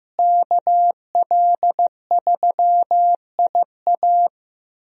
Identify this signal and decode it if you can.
Morse code, heard as a keyed tone, about 15 wpm, 700 Hz. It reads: KL3IA
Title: USA Call Signs - 5 Characters - Repeated 3x - 15wpm